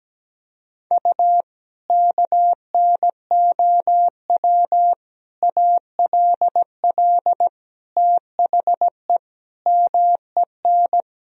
Morse code, heard as a keyed tone, about 17 wpm, 700 Hz. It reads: U KNOW ALL THE MEN
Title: Sentences from Top 300 Words and Abbreviations 17wpm